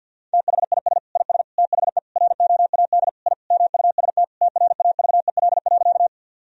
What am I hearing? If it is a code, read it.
Morse code, heard as a keyed tone, about 50 wpm, 700 Hz. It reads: THIS IS THE ROAD I MUST TRAVEL.
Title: Sentences from Top 1000 Words - Review Part 2 - 50wpm